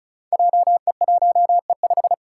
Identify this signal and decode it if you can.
Morse code, heard as a keyed tone, about 34 wpm, 700 Hz. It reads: JE1E5
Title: Five Challenging Letter-Number Combinations 35wpm